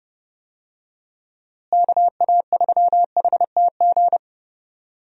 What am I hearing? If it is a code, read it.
Morse code, heard as a keyed tone, about 31 wpm, 700 Hz. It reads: KA3HTG
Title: USA Call Signs - 6 Characters - Repeated 3x - 30wpm